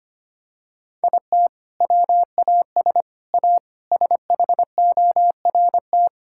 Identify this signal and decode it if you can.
Morse code, heard as a keyed tone, about 25 wpm, 700 Hz. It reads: IT WAS A SHORT
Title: Sentences from Top 1600 Words 25wpm